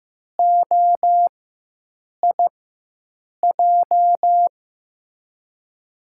Morse code, heard as a keyed tone, about 15 wpm, spaced as for 7 wpm, 700 Hz. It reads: OIJ
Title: USA Call Signs - 6 Characters - 4x Character Spacing - 15wpm